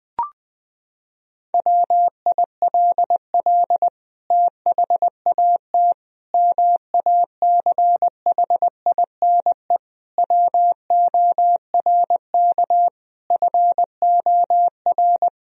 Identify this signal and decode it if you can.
Morse code, heard as a keyed tone, about 21 wpm, 700 Hz. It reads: WILL THAT MACHINE WORK FOR
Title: Sentences from Top 1000 Words - Review Part 2 - 20wpm